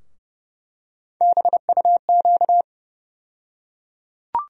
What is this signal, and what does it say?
Morse code, heard as a keyed tone, about 30 wpm, 700 Hz. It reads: BUQ
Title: USA Call Sign Postfixes - 3 Letters 30wpm